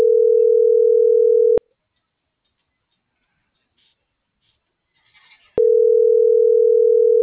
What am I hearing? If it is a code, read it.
Ringback tone (North America), 440 + 480 Hz.